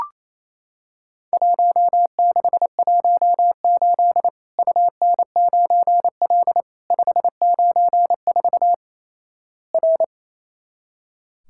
Morse code, heard as a keyed tone, about 28 wpm, 700 Hz. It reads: 1618 UN9L 594 R
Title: Worked All Europe DX Contest - QTC Exchange - 28wpm